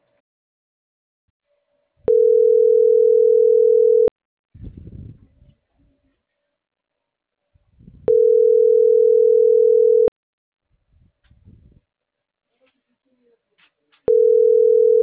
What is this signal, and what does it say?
Ringback tone (North America), 440 + 480 Hz.